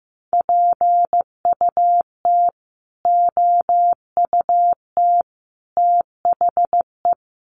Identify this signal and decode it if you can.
Morse code, heard as a keyed tone, about 15 wpm, 700 Hz. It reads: PUT OUT THE
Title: Sentences from Top 1400 Words 15wpm